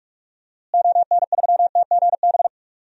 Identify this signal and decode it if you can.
Morse code, heard as a keyed tone, about 43 wpm, 700 Hz. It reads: ON3TGB
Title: DXCC-100 Call Signs - 6 Characters - Repeated 2x - 45wpm